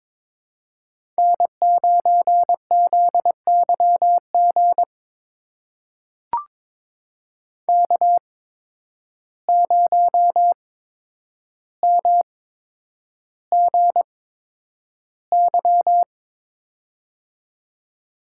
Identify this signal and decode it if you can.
Morse code, heard as a keyed tone, about 22 wpm, spaced as for 9 wpm, 700 Hz. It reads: N9ZYG K0MGY
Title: USA Call Signs - 5 Characters - 8x Character Spacing - 22wpm